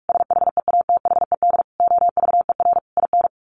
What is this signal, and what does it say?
Morse code, heard as a keyed tone, about 46 wpm, 700 Hz. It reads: SHEATHED OVER IN
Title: A Sound Of Thunder 45wpm